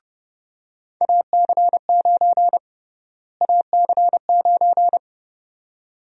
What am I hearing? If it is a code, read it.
Morse code, heard as a keyed tone, about 31 wpm, 700 Hz. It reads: AC9 AC9
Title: USA Call Sign Prefixes - Repeated 2x - 30wpm